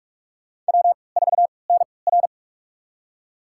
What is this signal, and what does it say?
Morse code, heard as a keyed tone, about 43 wpm, spaced as for 25 wpm, 690 Hz. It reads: W4NR
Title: USA Call Signs - 4 Characters - 3x Character Spacing - 45wpm